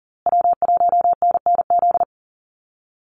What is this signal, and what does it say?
Morse code, heard as a keyed tone, about 40 wpm, 700 Hz. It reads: W1NNZ
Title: USA Call Signs Repeated 3x 40wpm